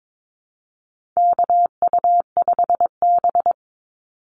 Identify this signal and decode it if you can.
Morse code, heard as a keyed tone, about 22 wpm, 700 Hz. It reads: KU5B
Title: Call Signs - 22wpm